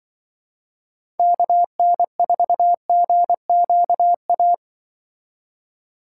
Morse code, heard as a keyed tone, about 23 wpm, 700 Hz. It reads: KN4GQA
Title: Speed Racing - USA 2x3 Call Signs 20wpm